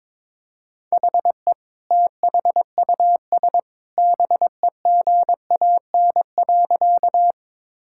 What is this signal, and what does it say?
Morse code, heard as a keyed tone, about 22 wpm, 700 Hz. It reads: HE THUS BEGAN.
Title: Sentences from Top 1000 Words - Review Part 5 - 22wpm